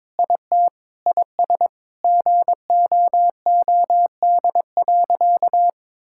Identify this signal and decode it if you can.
Morse code, heard as a keyed tone, about 22 wpm, 700 Hz. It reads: IT IS GOOD.
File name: practice / 22wpm / Sentences with 3-Word Phrases - 22wpm